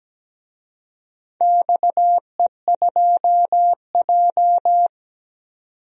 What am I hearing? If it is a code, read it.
Morse code, heard as a keyed tone, about 17 wpm, 700 Hz. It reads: XE2J
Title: DXCC-100 Call Signs - 4 Characters - 17wpm